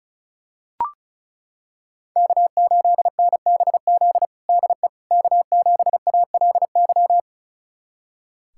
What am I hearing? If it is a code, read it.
Morse code, heard as a keyed tone, about 36 wpm, 700 Hz. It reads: K8NBZ DE K7ALY
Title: USA Callsign DE Callsign - 5 Characters - 35wpm